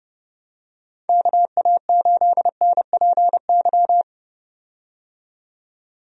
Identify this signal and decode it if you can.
Morse code, heard as a keyed tone, about 30 wpm, 700 Hz. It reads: KA8NPY
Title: USA Call Signs - 6 Characters - 30wpm